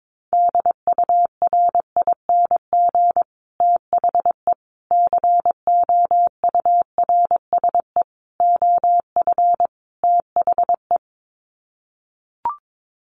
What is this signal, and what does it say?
Morse code, heard as a keyed tone, about 22 wpm, 700 Hz. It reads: DURING THE COURSE OF THE
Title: Sets of 5 Words Encore 22wpm